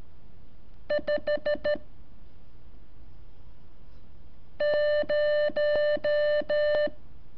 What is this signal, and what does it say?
Morse code, heard as a keyed tone, about 13 wpm, 590 Hz. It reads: 5 0
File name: curso / ejercicio 20